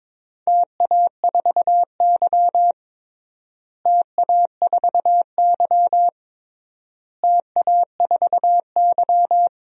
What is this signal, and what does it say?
Morse code, heard as a keyed tone, about 22 wpm, 700 Hz. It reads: TA4Y TA4Y TA4Y
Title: DXCC-100 Call Signs - 4 Characters - Repeated 3x - 22wpm